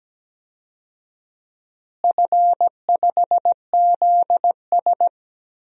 Morse code, heard as a keyed tone, about 17 wpm, 700 Hz. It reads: F5ZS
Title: DXCC-100 Call Signs - 4 Characters - Repeated 2x - 17wpm